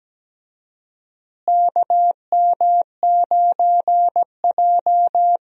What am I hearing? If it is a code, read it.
Morse code, heard as a keyed tone, about 17 wpm, 700 Hz. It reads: KM9J
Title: USA Call Signs - 4 Characters - Repeated 3x - 17wpm